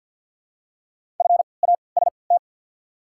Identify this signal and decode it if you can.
Morse code, heard as a keyed tone, about 51 wpm, spaced as for 28 wpm, 690 Hz. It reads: FAST
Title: 3x Character Spacing - Top 500 Words - 50wpm